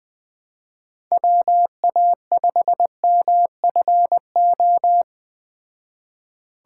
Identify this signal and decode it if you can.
Morse code, heard as a keyed tone, about 20 wpm, 700 Hz. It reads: WA5MFO